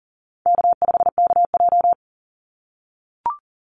Morse code, heard as a keyed tone, about 37 wpm, 700 Hz. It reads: K5KJ
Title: Easy Call Signs - 40wpm